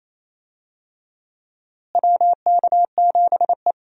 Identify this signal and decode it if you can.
Morse code, heard as a keyed tone, about 28 wpm, 700 Hz. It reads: WK7E
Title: USA Call Signs - 4 Characters - Repeated 3x - 28wpm